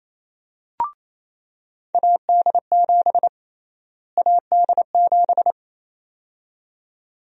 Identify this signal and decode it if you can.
Morse code, heard as a keyed tone, about 28 wpm, 700 Hz. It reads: AD7 AD7
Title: USA Call Sign Prefixes - Repeated 2x - 28wpm